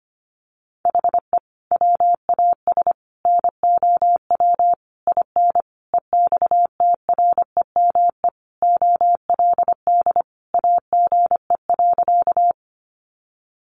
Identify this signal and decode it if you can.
Morse code, heard as a keyed tone, about 25 wpm, 700 Hz. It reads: HE WAS NOW IN EXTREME OLD AGE.
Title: Sentences from Top 1300 Words 25wpm